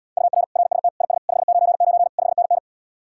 Morse code, heard as a keyed tone, about 77 wpm, 700 Hz. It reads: UR RST IS 599 5NN